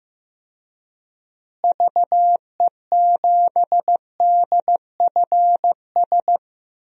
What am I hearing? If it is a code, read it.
Morse code, heard as a keyed tone, about 15 wpm, 700 Hz. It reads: VE7DFS
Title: DXCC-100 Call Signs - 6 Characters - Repeated 3x - 15wpm